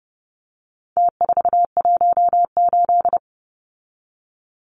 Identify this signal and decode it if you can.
Morse code, heard as a keyed tone, about 30 wpm, 700 Hz. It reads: T418
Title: Numbers - Contest Style - 4-Digit 30wpm